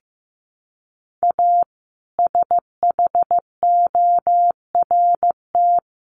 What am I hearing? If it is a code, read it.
Morse code, heard as a keyed tone, about 15 wpm, 700 Hz. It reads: A SHORT
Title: Sets of 5 Words Encore 15wpm